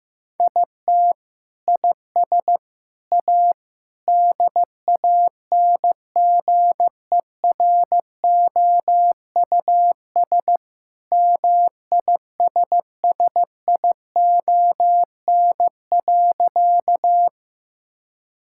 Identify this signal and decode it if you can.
Morse code, heard as a keyed tone, about 15 wpm, 700 Hz. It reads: IT IS A DANGEROUS MISSION.
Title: Sentences from Top 1600 Words 15wpm